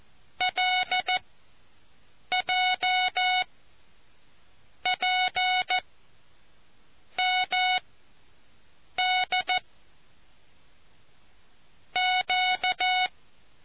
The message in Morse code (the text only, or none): LJ P MD Q